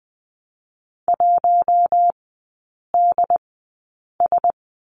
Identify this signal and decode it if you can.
Morse code, heard as a keyed tone, about 20 wpm, spaced as for 8 wpm, 700 Hz. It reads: 1DS
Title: 067 Lesson 12e - Letter d - Three Characters 20wpm